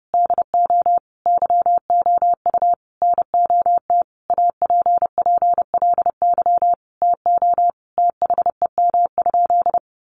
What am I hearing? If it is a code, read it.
Morse code, heard as a keyed tone, about 28 wpm, 700 Hz. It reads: DO YOU NOT APPLY TO THEM?
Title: Sentences from Top 1200 Words 30wpm